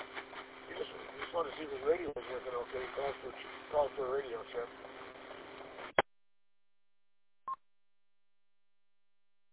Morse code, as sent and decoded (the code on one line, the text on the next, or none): none